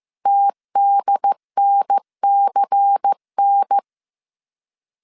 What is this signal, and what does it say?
Morse code, heard as a keyed tone, about 15 wpm, 800 Hz. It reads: TDNCN
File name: CW / DET / 1 6 1500C S C 14wpm 2951